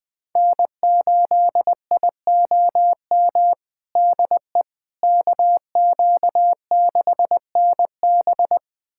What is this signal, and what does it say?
Morse code, heard as a keyed tone, about 21 wpm, 700 Hz. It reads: N8IOM DE KQ6NB
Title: USA Callsign DE Callsign - 5 Characters - 20wpm